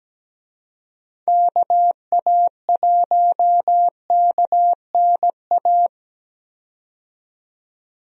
Morse code, heard as a keyed tone, about 17 wpm, 700 Hz. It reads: KA1KNA
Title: USA Call Signs - 6 Characters - 17wpm